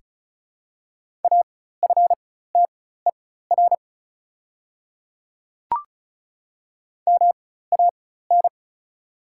Morse code, heard as a keyed tone, about 36 wpm, spaced as for 16 wpm, 700 Hz. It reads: AFTER MAN
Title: 4x Character Spacing - Top 300 Words - 35wpm